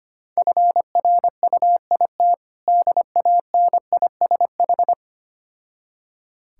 Morse code, heard as a keyed tone, about 25 wpm, 690 Hz. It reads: FRUIT DANISH
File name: practice / 25wpm / Difficult Words with Hints - 25wpm